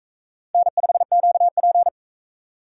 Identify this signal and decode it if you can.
Morse code, heard as a keyed tone, about 38 wpm, 700 Hz. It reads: N5QP